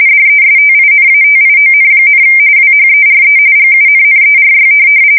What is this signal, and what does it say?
RTTY, about 45.45 baud. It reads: DESTINATION, WHERE GUNNERY
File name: fsk170 / USS-New-Jersey